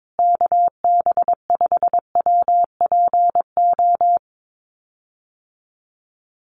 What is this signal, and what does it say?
Morse code, heard as a keyed tone, about 22 wpm, 700 Hz. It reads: KB5WPO
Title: USA Call Signs Repeated 2x 22wpm